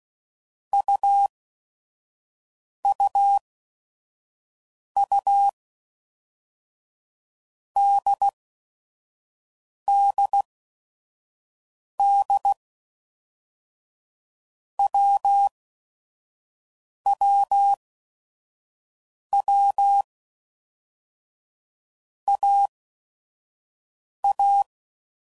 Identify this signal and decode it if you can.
Morse code, heard as a keyed tone, about 16 wpm, 800 Hz. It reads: U U U D D D W W W A A
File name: Basic Letter Groups By Blocks / Groups@5 / AWUNDG@5 f15